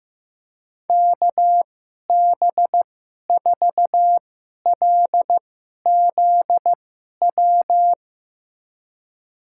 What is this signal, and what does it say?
Morse code, heard as a keyed tone, about 15 wpm, spaced as for 11 wpm, 700 Hz. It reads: KB4LZW
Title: USA Call Signs - 6 Characters - 2x Character Spacing - 15wpm